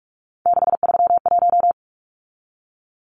Morse code, heard as a keyed tone, about 44 wpm, 700 Hz. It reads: 631
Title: Numbers - 3-Digit 45wpm